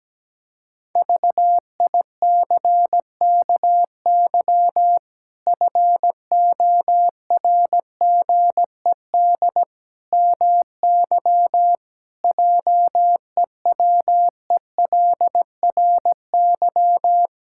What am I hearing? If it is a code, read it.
Morse code, heard as a keyed tone, about 17 wpm, 700 Hz. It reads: VICKY FORGED MY JEWELRY
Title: Pangram Challenge v2 - 17wpm